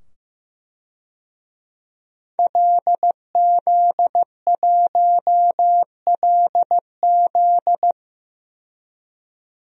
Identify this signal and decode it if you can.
Morse code, heard as a keyed tone, about 15 wpm, 700 Hz. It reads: LZ1LZ
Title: DXCC-100 Call Signs - 5 Characters - 15wpm